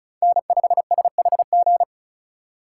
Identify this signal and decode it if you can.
Morse code, heard as a keyed tone, about 34 wpm, 700 Hz. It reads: N5SHG
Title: USA Call Signs - 5 Characters - 3x Character Spacing - 35wpm